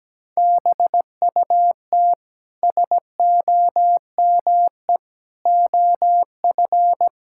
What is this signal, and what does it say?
Morse code, heard as a keyed tone, about 17 wpm, 700 Hz. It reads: BUT SOME OF
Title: Sentences from Top 1600 Words 17wpm